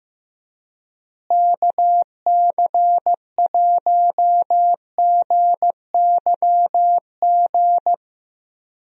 Morse code, heard as a keyed tone, about 15 wpm, 700 Hz. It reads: KC1GYG
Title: USA Call Signs - 6 Characters - Repeated 2x - 15wpm